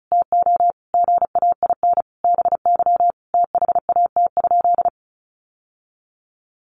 Morse code, heard as a keyed tone, about 34 wpm, 700 Hz. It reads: TO GAIN BY THAT?
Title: Sentences from Top 1100 Words 35wpm